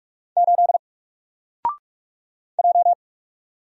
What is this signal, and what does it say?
Morse code, heard as a keyed tone, about 44 wpm, 700 Hz. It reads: 8 J
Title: Single Letter-Number 45wpm